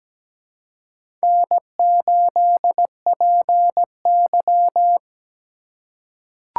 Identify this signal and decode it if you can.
Morse code, heard as a keyed tone, about 17 wpm, 700 Hz. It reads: N8PY
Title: USA Call Signs - 2x Character Spacing - 17wpm